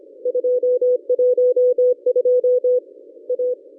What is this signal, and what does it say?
Morse code, heard as a keyed tone, about 25 wpm, 500 Hz. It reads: 212 A
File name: contests / ss-cw-k5tr-op-n5rz -1100